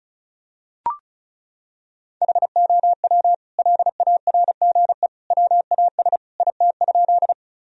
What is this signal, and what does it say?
Morse code, heard as a keyed tone, about 36 wpm, 700 Hz. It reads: HOW LARGE WAS IT?
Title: Sentences from Top 1000 Words - Review Part 1 - 35wpm